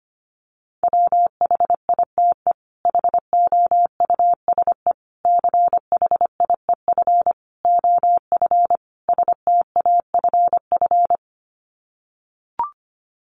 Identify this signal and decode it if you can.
Morse code, heard as a keyed tone, about 25 wpm, 700 Hz. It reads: WHITE HOUSE CHIEF OF STAFF
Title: Sets of 5 Words Encore 25wpm